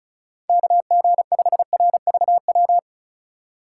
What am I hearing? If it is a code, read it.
Morse code, heard as a keyed tone, about 34 wpm, 700 Hz. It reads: KG5RVW
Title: USA Call Signs - 6 Characters - 4x Character Spacing - 35wpm